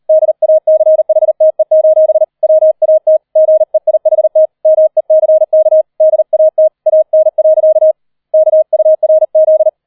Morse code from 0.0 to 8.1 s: DACHTE: WAT GEIHT MECK DAT AN.